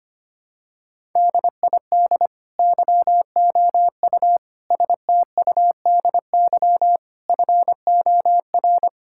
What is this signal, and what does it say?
Morse code, heard as a keyed tone, about 25 wpm, 700 Hz. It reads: DID YOU STUDY FOR
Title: Sentences from Top 1000 Words - Review Part 6 - 25wpm